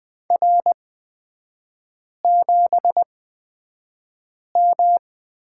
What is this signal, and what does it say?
Morse code, heard as a keyed tone, about 20 wpm, spaced as for 5 wpm, 700 Hz. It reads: R7M
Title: 243 Lesson 39e - Letter z - Three Characters 20wpm